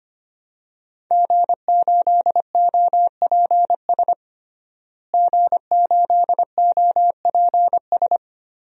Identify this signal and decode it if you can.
Morse code, heard as a keyed tone, about 25 wpm, 700 Hz. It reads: G8OPS G8OPS
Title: DXCC-100 Call Signs - 5 Characters - Repeated 2x - 25wpm